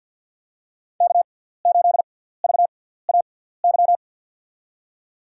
Morse code, heard as a keyed tone, about 53 wpm, spaced as for 17 wpm, 700 Hz. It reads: K8VAY
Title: USA Call Signs - 6x Character Spacing - 50wpm